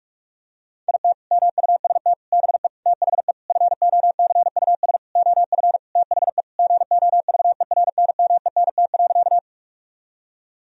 Morse code, heard as a keyed tone, about 43 wpm, 700 Hz. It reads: IT MUST BE THE FOCUS OF THE GOVERNMENT.
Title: Sentences from Top 1000 Words - Review Part 4 - 45wpm